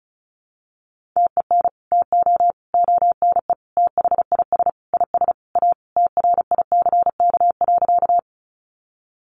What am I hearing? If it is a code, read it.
Morse code, heard as a keyed tone, about 34 wpm, 700 Hz. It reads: TEN TO ONE THIS IS A TRICK.
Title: Sentences from Top 1500 Words 35wpm